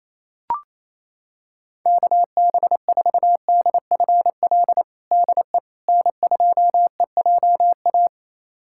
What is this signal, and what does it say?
Morse code, heard as a keyed tone, about 28 wpm, 700 Hz. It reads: KB4DFL DE N2EJA
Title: USA Callsign DE Callsign 28wpm